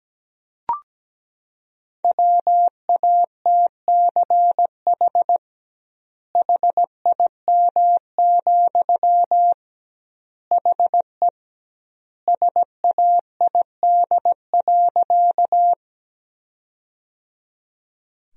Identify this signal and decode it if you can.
Morse code, heard as a keyed tone, about 17 wpm, 700 Hz. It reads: WATCH HIM, HE SAID.
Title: 2x Word-Spacing - Sentences from Top 300 Words - 17wpm